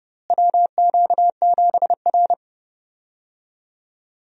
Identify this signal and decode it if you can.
Morse code, heard as a keyed tone, about 29 wpm, 700 Hz. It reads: WQ7R
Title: USA Call Signs - 4 Characters - Repeated 3x - 30wpm